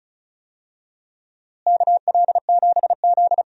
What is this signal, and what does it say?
Morse code, heard as a keyed tone, about 34 wpm, 700 Hz. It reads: KL7Z
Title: DXCC-100 Call Signs - 4 Characters - Repeated 2x - 35wpm